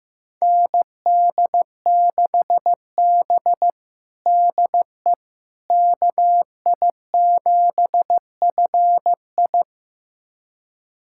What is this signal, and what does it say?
Morse code, heard as a keyed tone, about 15 wpm, 700 Hz. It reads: ND6B DE KI7FI
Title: USA Callsign DE Callsign 15wpm